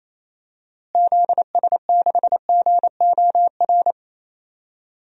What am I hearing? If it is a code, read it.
Morse code, heard as a keyed tone, about 28 wpm, 700 Hz. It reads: ZS6GOR
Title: DXCC-100 Call Signs - 6 Characters - Repeated 3x - 28wpm